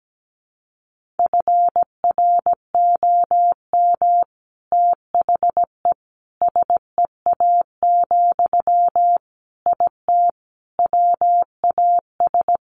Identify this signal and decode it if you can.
Morse code, heard as a keyed tone, about 17 wpm, 700 Hz. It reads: FROM THE SEA, IT WAS
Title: Sentences from Top 1500 Words 17wpm